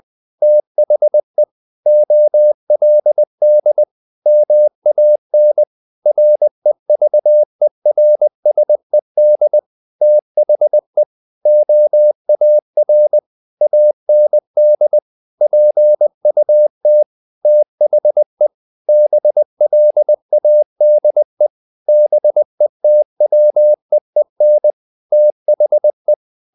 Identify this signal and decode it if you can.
Morse code, heard as a keyed tone, about 20 wpm, 600 Hz. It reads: THE OLD MAN REVERSED THE OAR AND PUT THE BLADE BETWEEN THE
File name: the old man and the sea part 2 20wpm